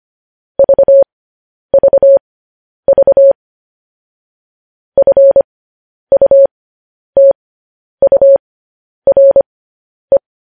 Morse code, heard as a keyed tone, about 25 wpm, spaced as for 10 wpm, 550 Hz. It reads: VVV FUTURE
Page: news-headlines-in-morse-code-at-10-wpm